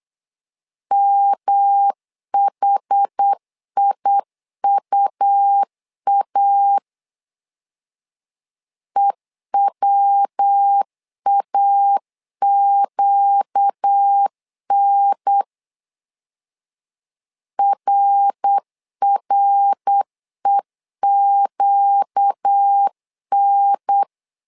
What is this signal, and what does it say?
Morse code, heard as a keyed tone, about 8 wpm, 800 Hz. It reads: MHIUA EWAQN RREQN
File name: LEZIONI / 4 Q 8wpm 2732